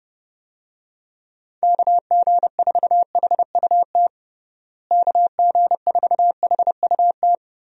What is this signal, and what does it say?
Morse code, heard as a keyed tone, about 30 wpm, 700 Hz. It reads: KG4HUT KG4HUT
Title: USA Call Signs - 6 Characters - Repeated 3x - 30wpm